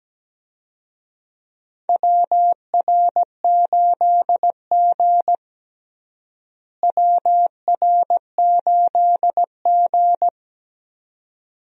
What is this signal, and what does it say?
Morse code, heard as a keyed tone, about 17 wpm, 700 Hz. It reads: WR8G WR8G
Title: USA Call Signs - 4 Characters - Repeated 2x - 17wpm